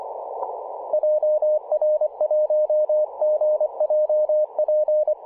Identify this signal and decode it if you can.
Morse code, heard as a keyed tone, about 24 wpm, 590 Hz, with radio noise. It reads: JR1GJP
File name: 5W0MA 18.077MHz CW